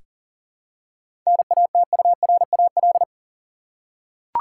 Morse code, heard as a keyed tone, about 40 wpm, 700 Hz. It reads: NATURAL
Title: Seven Letter Words - 40wpm